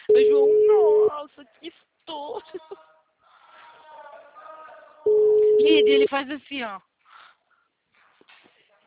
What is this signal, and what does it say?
Ringback tone (Europe, 425 Hz), with speech.
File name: out- -2059- - - .39295